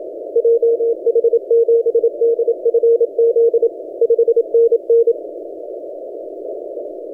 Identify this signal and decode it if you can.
Morse code, heard as a keyed tone, about 27 wpm, 460 Hz. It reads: JH7DFZ 5NN
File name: H44KW 17 CW